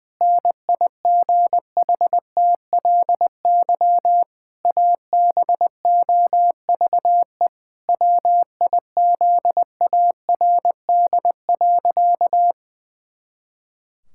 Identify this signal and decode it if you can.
Morse code, heard as a keyed tone, about 20 wpm, 700 Hz. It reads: NIGHTLY ABOVE WIZARD.
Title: Pangram Challenge v2 - 20wpm